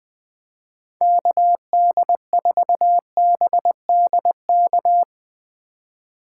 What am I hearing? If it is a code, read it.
Morse code, heard as a keyed tone, about 22 wpm, 700 Hz. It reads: KD4BDK
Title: Speed Racing - USA 2x3 Call Signs 20wpm